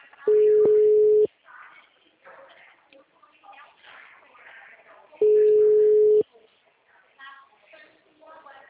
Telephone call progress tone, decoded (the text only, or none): ringback tone